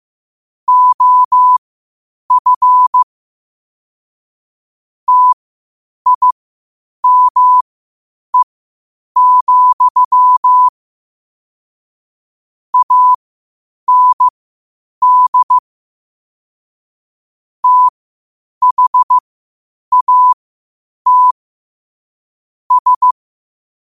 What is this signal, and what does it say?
Morse code, heard as a keyed tone, about 15 wpm, spaced as for 8 wpm, 1000 Hz. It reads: OF TIME, AND THAT S
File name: qotdfiles / 8wpm